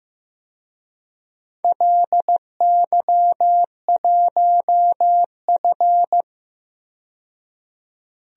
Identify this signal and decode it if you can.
Morse code, heard as a keyed tone, about 15 wpm, 700 Hz. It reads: LY1F
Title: DXCC-100 Call Signs - 4 Characters - 15wpm